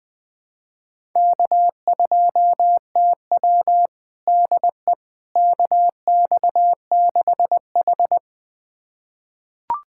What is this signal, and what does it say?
Morse code, heard as a keyed tone, about 21 wpm, 700 Hz. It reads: K2TW DE KX6H
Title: USA Callsign DE Callsign - 4 Characters - 20wpm